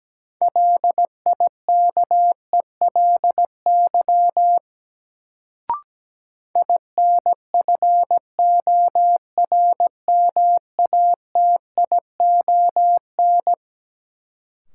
Morse code, heard as a keyed tone, about 17 wpm, 700 Hz. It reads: LIKELY INFORMATION
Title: Sentences from Top 600 Words - New Word Drill - 17wpm